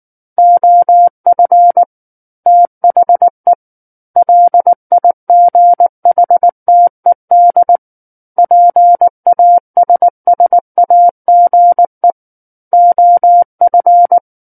Morse code, heard as a keyed tone, about 19 wpm, 700 Hz. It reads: OF THE LIGHTED PASSAGE OF
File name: Book One / 10-Chapter 10 - 19 WPM